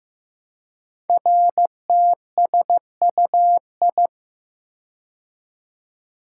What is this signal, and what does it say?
Morse code, heard as a keyed tone, about 15 wpm, 700 Hz. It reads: RTSUI
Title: Code Groups - 5 Letters - 15wpm